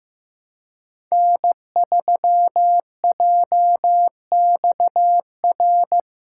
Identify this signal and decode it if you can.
Morse code, heard as a keyed tone, about 15 wpm, 700 Hz. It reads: N3JXR
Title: USA Call Signs - 5 Characters - 4x Character Spacing - 15wpm